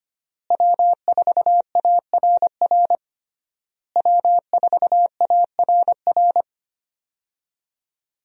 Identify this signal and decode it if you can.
Morse code, heard as a keyed tone, about 25 wpm, 700 Hz. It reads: W4ARR W4ARR
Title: USA Call Signs - 5 Characters - Repeated 3x - 25wpm